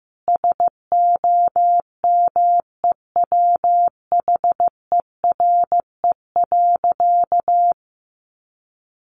Morse code, heard as a keyed tone, about 15 wpm, 700 Hz. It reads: SOMEWHERE.
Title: Sentences from Top 1400 Words 15wpm